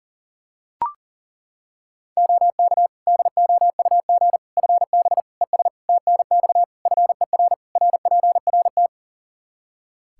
Morse code, heard as a keyed tone, about 40 wpm, 700 Hz. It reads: OK DOUG FB ES TNX FER RPRT
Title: Practice QSO - Exchange 2 - Info - 40wpm